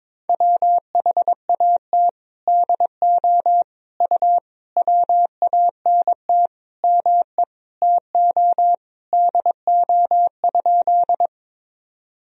Morse code, heard as a keyed tone, about 21 wpm, 700 Hz. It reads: WHAT DO U WANT ME TO DO?
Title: Sentences from Top 200 Words and Abbreviations 22wpm